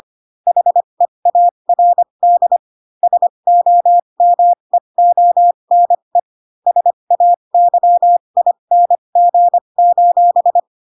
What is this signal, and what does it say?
Morse code, heard as a keyed tone, about 25 wpm, spaced as for 22 wpm, 700 Hz. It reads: HEARD SOMEONE SAYING:
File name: Pinocchio / Pinocchio ch4 18 25 ik1hge